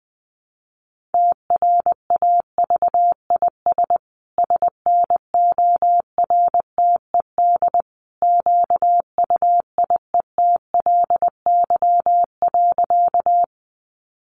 Morse code, heard as a keyed tone, about 20 wpm, 700 Hz. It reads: TRAVIS SNORTED QUIETLY.
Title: A Sound Of Thunder 20wpm